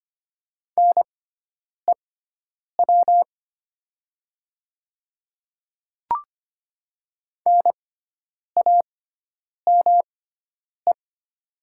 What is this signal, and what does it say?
Morse code, heard as a keyed tone, about 25 wpm, spaced as for 9 wpm, 700 Hz. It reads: NEW NAME
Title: 6x Character Spacing - Top 200 Words - 25wpm